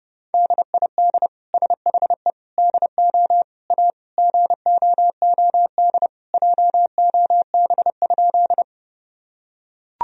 Morse code, heard as a keyed tone, about 30 wpm, 700 Hz. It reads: DID SHE DO A GOOD JOB?
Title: Sentences from Top 1000 Words - Review Part 3 - 30wpm